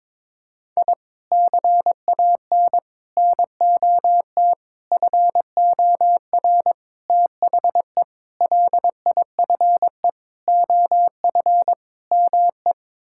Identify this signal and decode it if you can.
Morse code, heard as a keyed tone, about 22 wpm, 700 Hz. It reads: I CAN NOT FOR THE LIFE OF ME
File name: practice / 22wpm / Sets of 8 Words - 22wpm